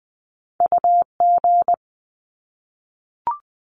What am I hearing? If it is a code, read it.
Morse code, heard as a keyed tone, about 20 wpm, 700 Hz. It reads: UG
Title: Top Two Letter Combinations in Words 20wpm